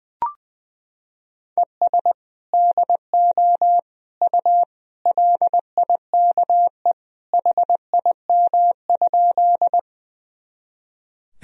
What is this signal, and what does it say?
Morse code, heard as a keyed tone, about 20 wpm, 700 Hz. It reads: ES DO U LIKE HIM?
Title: Sentences from Top 100 Words and Abbreviations 20wpm